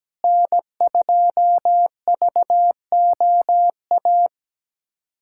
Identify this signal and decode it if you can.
Morse code, heard as a keyed tone, about 17 wpm, 700 Hz. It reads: N2VOA
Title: USA Callsign DE Callsign - 5 Characters - 17wpm